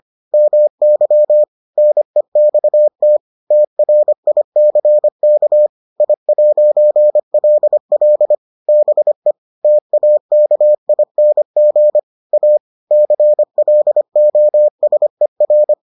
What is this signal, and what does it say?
Morse code, heard as a keyed tone, about 25 wpm, 600 Hz. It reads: MY NEXT TRICK I'LL BE TAKING A CLOSER